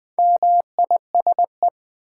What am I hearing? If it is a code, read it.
Morse code, heard as a keyed tone, about 20 wpm, 700 Hz. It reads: MISE
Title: Build A Word - 2 Letters At A Time - 6-Letter Words - 20wpm